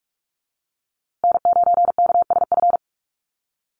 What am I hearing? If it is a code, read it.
Morse code, heard as a keyed tone, about 44 wpm, 700 Hz. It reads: N9KSF
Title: USA Call Signs Repeated 2x 45wpm